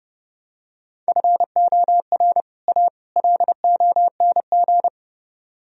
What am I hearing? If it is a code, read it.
Morse code, heard as a keyed tone, about 30 wpm, 700 Hz. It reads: FOR A LONG